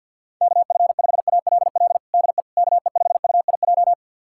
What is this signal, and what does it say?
Morse code, heard as a keyed tone, about 50 wpm, 700 Hz. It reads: KF5ALL DE KE5UIJ
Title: USA Callsign DE Callsign 50wpm